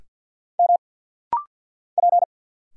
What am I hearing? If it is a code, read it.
Morse code, heard as a keyed tone, about 50 wpm, 700 Hz. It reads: M P
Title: Single Letters - Rapid-Fire 50wpm